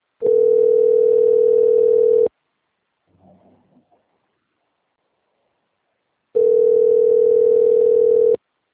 Ringback tone (North America), 440 + 480 Hz.